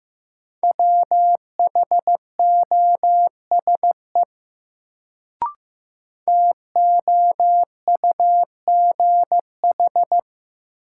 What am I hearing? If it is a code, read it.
Morse code, heard as a keyed tone, about 15 wpm, 700 Hz. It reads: WHOSE TOUGH
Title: Sentences from Top 1200 Words - New Word Drill Alt - 15wpm